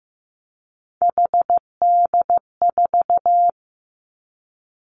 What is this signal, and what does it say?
Morse code, heard as a keyed tone, about 15 wpm, 700 Hz. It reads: HD4
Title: Three Challenging Letter-Number Combinations 15wpm